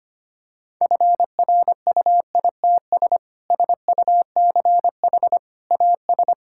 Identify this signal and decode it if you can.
Morse code, heard as a keyed tone, about 25 wpm, 700 Hz. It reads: FRUITS SUCH AS
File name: practice / 25wpm / Sentences with 3-Word Phrases - 25wpm